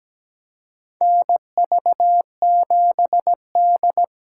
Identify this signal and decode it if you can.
Morse code, heard as a keyed tone, about 17 wpm, 700 Hz. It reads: NV7D
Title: USA Call Signs - 4 Characters - 2x Character Spacing - 17wpm